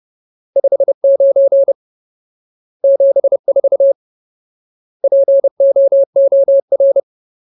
Morse code, heard as a keyed tone, about 30 wpm, 550 Hz. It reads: 59 74 POOR
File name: Ultimate Word List / 30WPM Ultimate